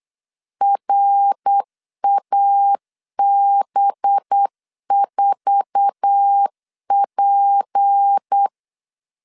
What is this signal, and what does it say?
Morse code, heard as a keyed tone, about 8 wpm, 800 Hz. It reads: RAB4P
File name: LEZIONI / 9 2 8wpm 3437